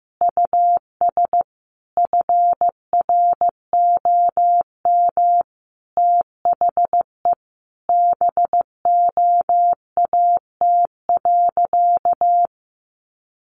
Morse code, heard as a keyed tone, about 15 wpm, 700 Hz. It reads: US FROM THE BOAT.
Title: Sentences from Top 1400 Words 15wpm